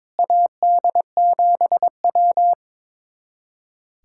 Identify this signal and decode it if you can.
Morse code, heard as a keyed tone, about 22 wpm, 700 Hz. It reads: AD7W
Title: USA Callsign DE Callsign - 4 Characters - 22wpm